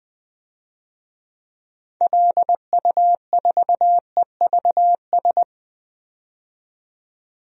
Morse code, heard as a keyed tone, about 20 wpm, 700 Hz. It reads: LU4EVS
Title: DXCC-100 Call Signs - 6 Characters - 20wpm